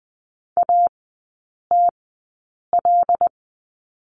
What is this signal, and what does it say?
Morse code, heard as a keyed tone, about 20 wpm, spaced as for 8 wpm, 700 Hz. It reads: ATL
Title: 074 Lesson 13e - Letter L - Three Characters 20wpm